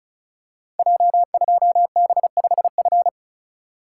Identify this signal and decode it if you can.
Morse code, heard as a keyed tone, about 36 wpm, 700 Hz. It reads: J2B5F
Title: Code Groups - 5 Letter:Numbers - 35wpm